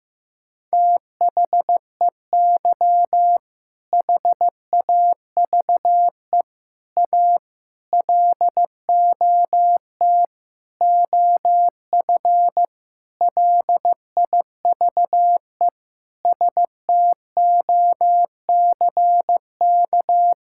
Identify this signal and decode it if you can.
Morse code, heard as a keyed tone, about 15 wpm, 700 Hz. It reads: THEY HAVE A LOT OF LIVE STOCK
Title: Sentences from Top 1000 Words - Review Part 5 - 15wpm